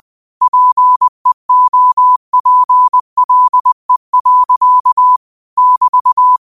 Morse code, heard as a keyed tone, about 20 wpm, 1000 Hz. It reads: PEOPLE. <BT>
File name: qotdfiles / 20wpm